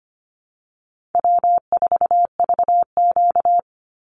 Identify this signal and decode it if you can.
Morse code, heard as a keyed tone, about 25 wpm, 700 Hz. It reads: W4VQ
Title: Easy Call Signs - 25wpm